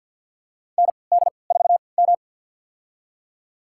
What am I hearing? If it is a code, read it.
Morse code, heard as a keyed tone, about 51 wpm, spaced as for 28 wpm, 700 Hz. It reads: ND4M
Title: USA Call Signs - 4 Characters - 3x Character Spacing - 50wpm